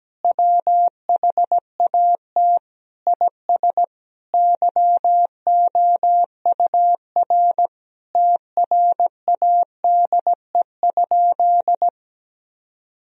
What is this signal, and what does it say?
Morse code, heard as a keyed tone, about 17 wpm, 700 Hz. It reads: WHAT IS YOUR TRADE?
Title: Sentences from Top 1000 Words - Review Part 4 - 17wpm